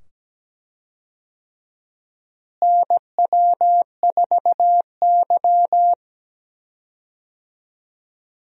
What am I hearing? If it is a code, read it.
Morse code, heard as a keyed tone, about 17 wpm, 700 Hz. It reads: NW4Y
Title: USA Call Signs - 4 Characters - Repeated 2x - 17wpm